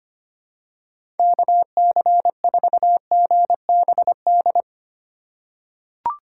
Morse code, heard as a keyed tone, about 25 wpm, 700 Hz. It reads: KC4GBD
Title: USA Call Signs - 6 Characters - 4x Character Spacing - 25wpm